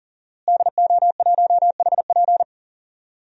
Morse code, heard as a keyed tone, about 40 wpm, 700 Hz. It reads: DO1HP
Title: DXCC-100 Call Signs - 5 Characters - Repeated 2x - 40wpm